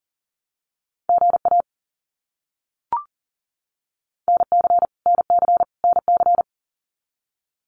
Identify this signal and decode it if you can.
Morse code, heard as a keyed tone, about 39 wpm, 700 Hz. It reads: GA NC NC NC
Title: US States Repeated 3x 40wpm